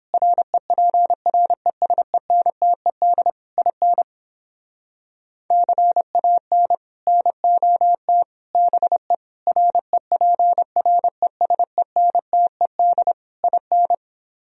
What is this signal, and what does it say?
Morse code, heard as a keyed tone, about 29 wpm, 700 Hz. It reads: REPRESENTED IN CAN NOT BE REPRESENTED IN